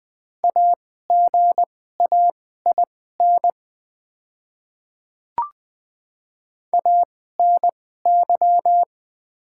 Morse code, heard as a keyed tone, about 20 wpm, spaced as for 14 wpm, 700 Hz. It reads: AGAIN ANY
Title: 2x Character Spacing - Top 300 Words - 20wpm